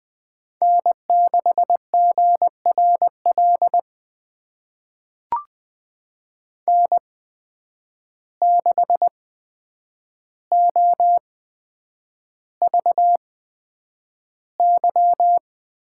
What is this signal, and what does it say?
Morse code, heard as a keyed tone, about 20 wpm, spaced as for 9 wpm, 700 Hz. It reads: N6GRL N6OVY
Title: USA Call Signs - 8x Character Spacing - 20wpm